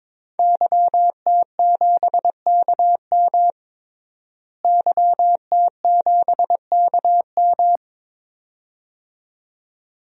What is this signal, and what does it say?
Morse code, heard as a keyed tone, about 22 wpm, 700 Hz. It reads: YT7KM YT7KM
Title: DXCC-100 Call Signs - 5 Characters - Repeated 2x - 22wpm